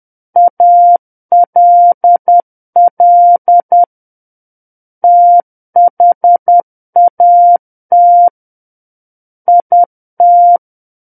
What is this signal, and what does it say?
Morse code, heard as a keyed tone, about 10 wpm, 700 Hz. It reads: ALL THAT IT
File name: Book One / 01-Chapter 1 - 10 WPM